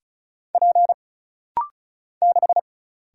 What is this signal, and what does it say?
Morse code, heard as a keyed tone, about 34 wpm, 700 Hz. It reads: P 6
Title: Single Letter-Number - Rapid-Fire 35wpm